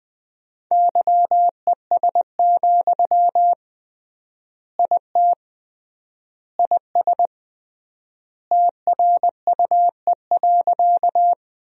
Morse code, heard as a keyed tone, about 21 wpm, 700 Hz. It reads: YES, IT IS TRUE.
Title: 3x Word-Spacing - Sentences from Top 500 Words - 20wpm